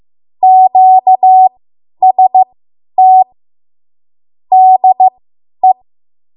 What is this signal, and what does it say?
Morse code, heard as a keyed tone, about 15 wpm, spaced as for 10 wpm, 750 Hz. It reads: QST DE
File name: Archive / 10 WPM / 10WPM